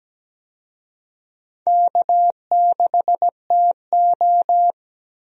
Morse code, heard as a keyed tone, about 17 wpm, 700 Hz. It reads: K6TO
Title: USA Call Signs - 4 Characters - Repeated 2x - 17wpm